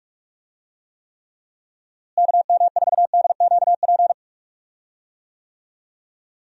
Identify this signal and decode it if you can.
Morse code, heard as a keyed tone, about 46 wpm, 700 Hz. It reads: KM4DQP
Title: USA Call Signs - 6 Characters - Repeated 3x - 45wpm